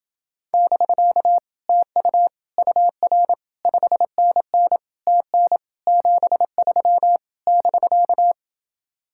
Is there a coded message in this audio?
Yes: Morse code from 0.5 to 8.3 s, heard as a keyed tone, about 28 wpm, 700 Hz. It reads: <BK> TU UR 5NN TN 73 <BK>